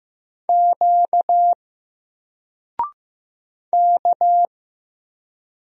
Morse code, heard as a keyed tone, about 15 wpm, 700 Hz. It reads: Q K
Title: Single Letter-Number Alt 15wpm